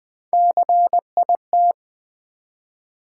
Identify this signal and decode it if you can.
Morse code, heard as a keyed tone, about 20 wpm, 700 Hz. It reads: CIT
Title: Build A Word - 1 Letter At A Time - 4-Letter Words - 20wpm